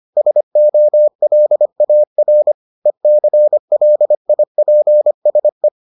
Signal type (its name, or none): Morse code